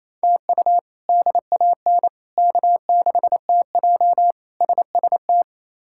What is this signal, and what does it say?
Morse code, heard as a keyed tone, about 27 wpm, 700 Hz. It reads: TU DAN K6TJ SST
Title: K1USN SST QSO Practice - 28wpm